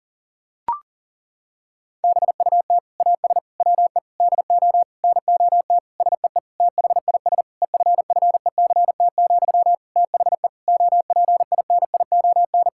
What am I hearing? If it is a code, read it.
Morse code, heard as a keyed tone, about 37 wpm, 700 Hz. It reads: BUT AS WE DO NOT SEE THIS EFFECT, THE OPINION
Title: Sentences from Top 1600 Words 40wpm